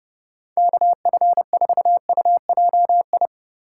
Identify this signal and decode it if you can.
Morse code, heard as a keyed tone, about 37 wpm, 700 Hz. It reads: KF4UJI
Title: Speed Racing - USA 2x3 Call Signs 30wpm